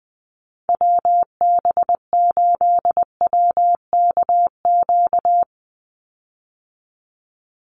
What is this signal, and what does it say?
Morse code, heard as a keyed tone, about 19 wpm, 700 Hz. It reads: WB8WKQ
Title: Call Signs - 20wpm